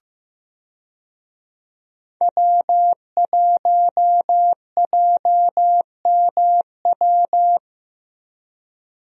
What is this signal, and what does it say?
Morse code, heard as a keyed tone, about 15 wpm, 700 Hz. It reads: W1JMW
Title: USA Call Signs - 5 Characters - Repeated 3x - 15wpm